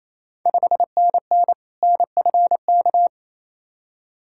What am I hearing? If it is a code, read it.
Morse code, heard as a keyed tone, about 27 wpm, 690 Hz. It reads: 5NN NFK
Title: Ontario QSO Party Exchange 28wpm